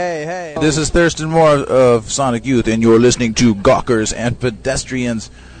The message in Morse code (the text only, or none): none